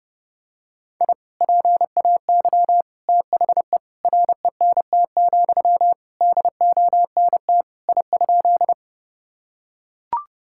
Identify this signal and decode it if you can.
Morse code, heard as a keyed tone, about 30 wpm, 700 Hz. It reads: I PAY THE RENT, DONT I?
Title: Sentences from Top 1000 Words - Review Part 5 - 30wpm